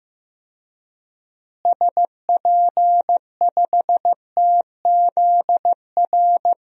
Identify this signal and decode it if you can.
Morse code, heard as a keyed tone, about 15 wpm, 700 Hz. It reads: SP5TZR
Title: DXCC-100 Call Signs - 6 Characters - Repeated 3x - 15wpm